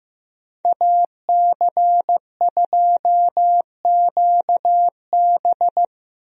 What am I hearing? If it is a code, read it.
Morse code, heard as a keyed tone, about 15 wpm, 700 Hz. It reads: AC2QB
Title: USA Call Signs - 5 Characters - Repeated 2x - 15wpm